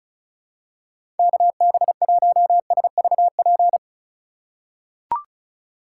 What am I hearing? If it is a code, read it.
Morse code, heard as a keyed tone, about 34 wpm, 700 Hz. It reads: KB1SVP
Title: USA Call Signs - 6 Characters - 4x Character Spacing - 35wpm